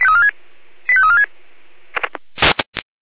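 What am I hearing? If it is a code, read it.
Five-tone selective call (ZVEI), heard as quick runs of five tones, tones of about 70 ms. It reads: 85346
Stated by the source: ZVEI five-tone call received by SDR.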